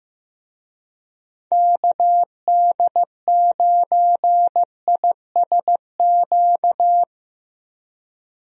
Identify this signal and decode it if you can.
Morse code, heard as a keyed tone, about 15 wpm, 700 Hz. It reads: KD9ISQ
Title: USA Call Signs - 6 Characters - Repeated 3x - 15wpm